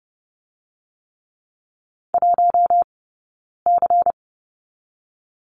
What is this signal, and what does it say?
Morse code, heard as a keyed tone, about 30 wpm, 700 Hz. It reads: 1 C
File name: Lesson 17d - Letter u - Two Characters 30wpm